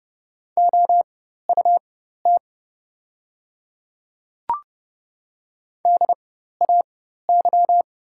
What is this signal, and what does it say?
Morse code, heard as a keyed tone, about 30 wpm, spaced as for 14 wpm, 690 Hz. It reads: OUT DAY
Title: 4x Character Spacing - Top 100 Words - 30wpm